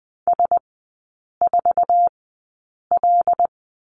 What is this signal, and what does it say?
Morse code, heard as a keyed tone, about 20 wpm, spaced as for 8 wpm, 700 Hz. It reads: S4L
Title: 074 Lesson 13e - Letter L - Three Characters 20wpm